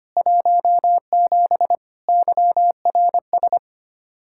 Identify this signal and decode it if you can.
Morse code, heard as a keyed tone, about 25 wpm, 700 Hz. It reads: 17 YRS
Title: K1USN SST QSO Practice 25wpm